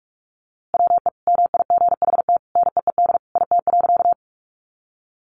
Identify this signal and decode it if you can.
Morse code, heard as a keyed tone, about 44 wpm, 700 Hz. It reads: WE MIGHT NEED IT.
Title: A Sound Of Thunder 45wpm